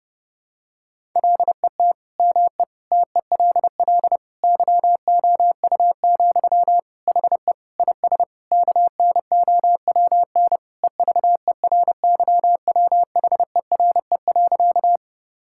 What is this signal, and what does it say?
Morse code, heard as a keyed tone, about 32 wpm, 700 Hz. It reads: LET ME TELL YOU, HE IS KNOWN EVERYWHERE.
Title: Sentences from Top 1600 Words 30wpm